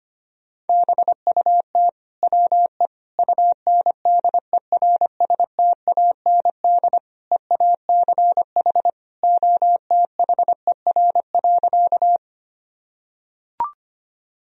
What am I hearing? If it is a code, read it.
Morse code, heard as a keyed tone, about 25 wpm, 700 Hz. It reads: BUT WE UNDERSTAND EACH OTHER.
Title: Sentences from Top 1000 Words - Review Part 1 - 25wpm